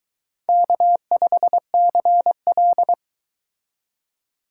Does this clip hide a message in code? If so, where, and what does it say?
Yes: Morse code from 0.5 to 2.9 s, heard as a keyed tone, about 23 wpm, 700 Hz. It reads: K5CL